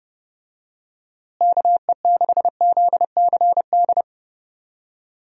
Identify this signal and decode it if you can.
Morse code, heard as a keyed tone, about 30 wpm, 700 Hz. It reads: KE6ZCD
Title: USA Call Signs - 6 Characters - Repeated 2x - 30wpm